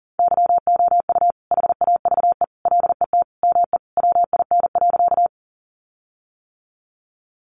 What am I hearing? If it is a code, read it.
Morse code, heard as a keyed tone, about 40 wpm, 700 Hz. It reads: YOU HAVE LET ME WIN.
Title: Sentences from Top 1500 Words 40wpm